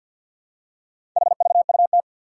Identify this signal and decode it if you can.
Morse code, heard as a keyed tone, about 46 wpm, 700 Hz. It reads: HVUT
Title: Four Challenging Letter-Number Combinations 50wpm